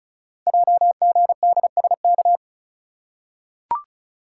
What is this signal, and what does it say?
Morse code, heard as a keyed tone, about 36 wpm, 690 Hz. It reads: JGDSK
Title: Code Groups - 5 Letter:Numbers - 35wpm